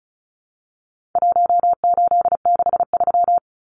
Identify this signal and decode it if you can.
Morse code, heard as a keyed tone, about 34 wpm, 700 Hz. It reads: 1863
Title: Numbers - Contest Style - 4-Digit 35wpm